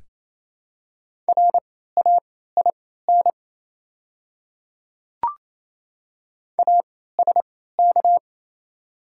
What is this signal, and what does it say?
Morse code, heard as a keyed tone, about 27 wpm, spaced as for 16 wpm, 700 Hz. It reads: RAIN ASK
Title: 3x Character Spacing - Top 300 Words - 28wpm